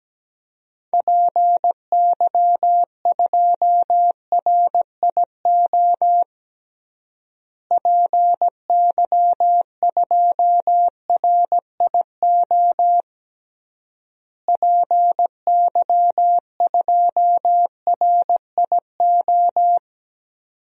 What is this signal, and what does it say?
Morse code, heard as a keyed tone, about 17 wpm, 700 Hz. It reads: PY2RIO PY2RIO PY2RIO
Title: DXCC-100 Call Signs - 6 Characters - Repeated 3x - 17wpm